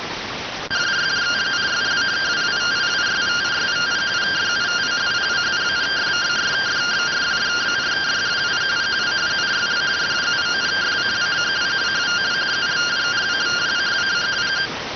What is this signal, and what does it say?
Two-tone teleprinter signal, about 170 Hz shift.